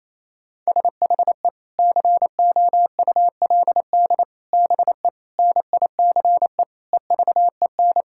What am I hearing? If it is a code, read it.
Morse code, heard as a keyed tone, about 28 wpm, 700 Hz. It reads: SHE COULD BE NICE EVEN
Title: Sentences from Top 1600 Words 28wpm